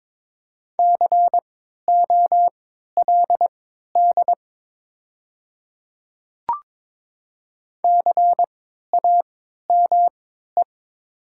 Morse code, heard as a keyed tone, about 22 wpm, spaced as for 12 wpm, 700 Hz. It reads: COLD CAME
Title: 3x Character Spacing - Top 500 Words - 22wpm